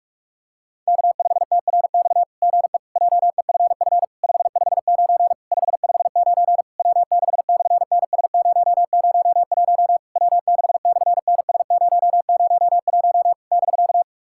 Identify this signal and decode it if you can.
Morse code, heard as a keyed tone, about 46 wpm, 700 Hz. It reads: K5TRX GE JEFF 559 559 W6/NS001 W6/NS001 <BK>
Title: SOTA QSO Practice 45wpm